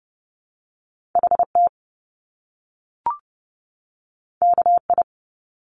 Morse code, heard as a keyed tone, about 30 wpm, 700 Hz. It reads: HT KI
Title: Top Two Letter Combinations in Words 30wpm